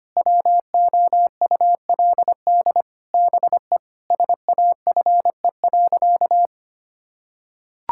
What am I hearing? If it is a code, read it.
Morse code, heard as a keyed tone, about 25 wpm, 700 Hz. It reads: WOULD BE SAFE.
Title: Sentences from Top 1600 Words 25wpm